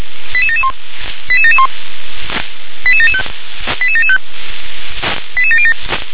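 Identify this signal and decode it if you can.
Five-tone selective call (ZVEI), heard as quick runs of five tones, tones of about 70 ms. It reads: 88771 88771 88775 88775 88707
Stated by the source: ZVEI five-tone call, code 88983, received by SDR.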